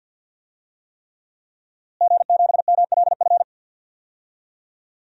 Morse code, heard as a keyed tone, about 50 wpm, 700 Hz. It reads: G7MLF
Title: DXCC-100 Call Signs - 5 Characters - Repeated 3x - 50wpm